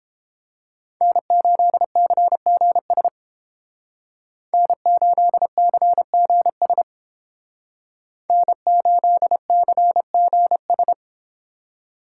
Morse code, heard as a keyed tone, about 30 wpm, 700 Hz. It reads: N8CGS N8CGS N8CGS